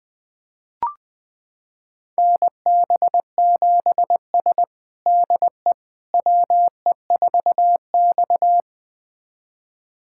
Morse code, heard as a keyed tone, about 20 wpm, 700 Hz. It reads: NB7S DE WE4X
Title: USA Callsign DE Callsign - 4 Characters - 20wpm